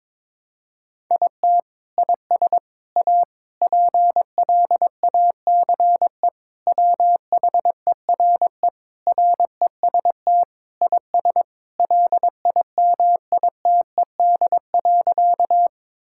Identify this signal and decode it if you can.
Morse code, heard as a keyed tone, about 22 wpm, 700 Hz. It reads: IT IS A PLACE WHERE REST IS LIMITED.
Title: Sentences from Top 1000 Words - Review Part 6 - 22wpm